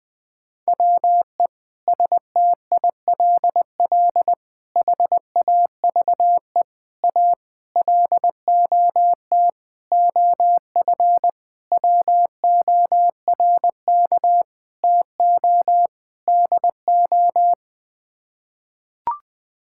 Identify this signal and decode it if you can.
Morse code, heard as a keyed tone, about 21 wpm, 700 Hz. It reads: WE STILL HAVE A LOT OF WORK TO DO
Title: Sets of 9 Words - 20wpm